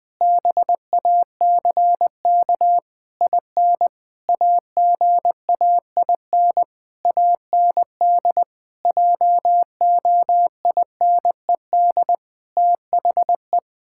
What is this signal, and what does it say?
Morse code, heard as a keyed tone, about 20 wpm, 700 Hz. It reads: BACK IN AGAIN AND JOINED THE
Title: Pangram Challenge v2 - 20wpm